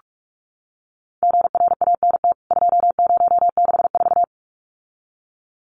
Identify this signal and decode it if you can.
Morse code, heard as a keyed tone, about 44 wpm, 700 Hz. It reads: GRANT 2064
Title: CWT Exchange 45wpm